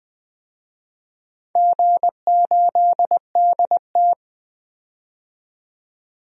Morse code, heard as a keyed tone, about 20 wpm, 700 Hz. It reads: G8DT
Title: DXCC-100 Call Signs - 4 Characters - 20wpm